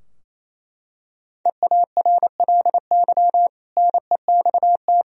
Morse code, heard as a keyed tone, about 28 wpm, 700 Hz. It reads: EARLY NEXT
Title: Sentences from Top 1000 Words - Review Part 4 - 28wpm